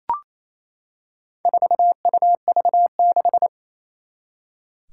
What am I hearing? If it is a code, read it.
Morse code, heard as a keyed tone, about 28 wpm, 700 Hz. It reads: 4UV6
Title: Four Challenging Letter-Number Combinations 28wpm